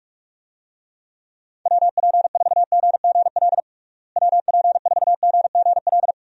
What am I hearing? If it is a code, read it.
Morse code, heard as a keyed tone, about 41 wpm, 700 Hz. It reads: WP4GGL WP4GGL
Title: USA Call Signs - 6 Characters - Repeated 3x - 45wpm